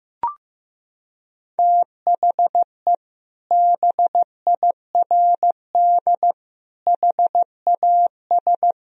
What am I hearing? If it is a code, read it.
Morse code, heard as a keyed tone, about 15 wpm, 700 Hz. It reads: THE BIRD HAS
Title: Sentences from Top 1000 Words - Review Part 2 - 15wpm